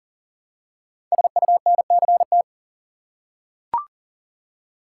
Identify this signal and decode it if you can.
Morse code, heard as a keyed tone, about 39 wpm, 700 Hz. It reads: SUNCT
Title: Code Groups - 5 Letter:Numbers - 40wpm